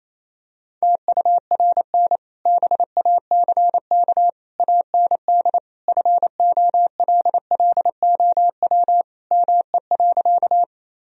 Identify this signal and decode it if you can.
Morse code, heard as a keyed tone, about 28 wpm, 700 Hz. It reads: TURN BACK AND FOLLOW ME.
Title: Sentences from Top 1000 Words - Review Part 1 - 28wpm